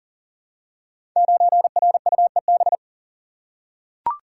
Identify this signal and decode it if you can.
Morse code, heard as a keyed tone, about 40 wpm, 690 Hz. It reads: 9RUEB
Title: Code Groups - 5 Letter:Numbers - 40wpm